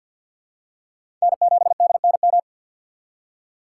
Morse code, heard as a keyed tone, about 48 wpm, 700 Hz. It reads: N7DNM
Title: USA Call Signs - 5 Characters - 4x Character Spacing - 50wpm